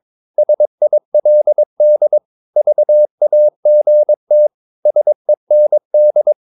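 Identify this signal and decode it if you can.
Morse code, heard as a keyed tone, about 22 wpm, 600 Hz. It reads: SILD VAGT SEND
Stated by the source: Morse practice of short Danish words at 22 wpm, from CW med Gnister.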